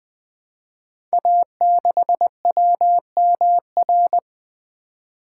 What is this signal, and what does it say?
Morse code, heard as a keyed tone, about 20 wpm, 700 Hz. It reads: A6WMR
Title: Code Groups - 5 Letter:Numbers - 20wpm